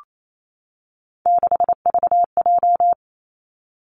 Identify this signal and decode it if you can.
Morse code, heard as a keyed tone, about 28 wpm, 700 Hz. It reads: 6VJ
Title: Three Challenging Letter-Number Combinations 28wpm